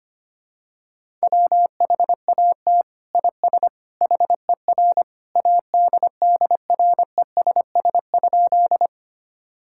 Morse code, heard as a keyed tone, about 25 wpm, 700 Hz. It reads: WHAT IS HER ADDRESS?
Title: Sentences from Top 1000 Words - Review Part 6 - 25wpm